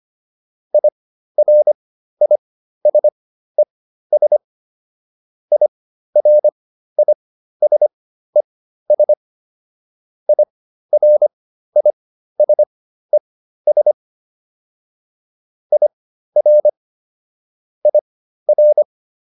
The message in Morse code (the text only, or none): IRISES IRISES IRISES IR IR